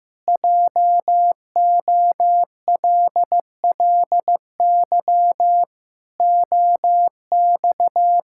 Morse code, heard as a keyed tone, about 15 wpm, 700 Hz. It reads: JOLLY OX
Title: Pangram Challenge v2 - 15wpm